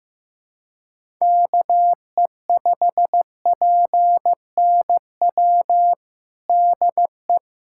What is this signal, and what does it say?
Morse code, heard as a keyed tone, about 16 wpm, 700 Hz. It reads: KE5PNW DE
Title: USA Callsign DE Callsign - 6 Characters - 15wpm